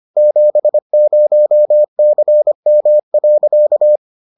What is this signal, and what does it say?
Morse code, heard as a keyed tone, about 25 wpm, 600 Hz. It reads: 70CM.